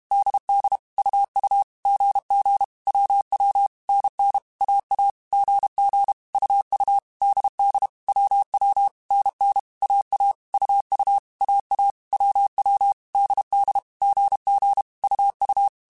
Morse code, heard as a keyed tone, about 32 wpm, 800 Hz. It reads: DD UU GG WW NN AA GG UU DD WW NN AA UU AA WW DD GG UU
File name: Basic Sets / Basic Letter Groups Collective7 / AWUNDG@30